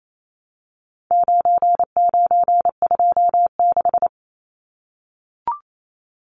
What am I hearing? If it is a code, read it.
Morse code, heard as a keyed tone, about 28 wpm, 700 Hz. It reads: 9926
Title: Numbers - 4-Digit 28wpm